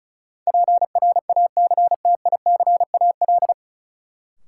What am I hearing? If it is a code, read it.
Morse code, heard as a keyed tone, about 34 wpm, 700 Hz. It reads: PRACTICAL
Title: Sentences from Top 1400 Words - New Word Drill - 35wpm